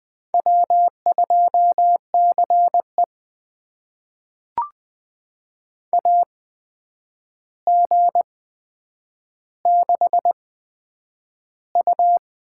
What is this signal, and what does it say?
Morse code, heard as a keyed tone, about 20 wpm, spaced as for 9 wpm, 700 Hz. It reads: W2CE AG6U
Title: USA Call Signs - 4 Characters - 8x Character Spacing - 20wpm